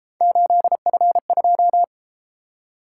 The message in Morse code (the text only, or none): 8F2